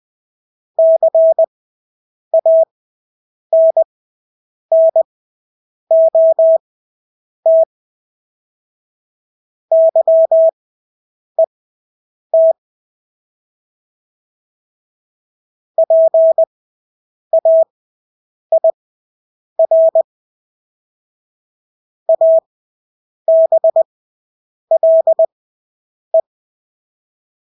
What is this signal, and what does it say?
Morse code, heard as a keyed tone, about 20 wpm, spaced as for 8 wpm, 650 Hz. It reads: CANNOT YET PAIR ABLE
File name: Engelske ord 20/8 wpm